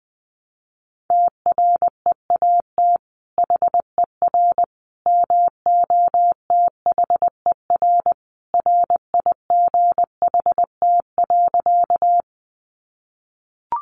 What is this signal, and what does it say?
Morse code, heard as a keyed tone, about 20 wpm, 700 Hz. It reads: TREAT HER MOTHER RIGHT.
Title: Sentences from Top 1300 Words 20wpm